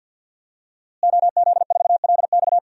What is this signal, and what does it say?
Morse code, heard as a keyed tone, about 51 wpm, 700 Hz. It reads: OZ4LX
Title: DXCC-100 Call Signs - 5 Characters - Repeated 2x - 50wpm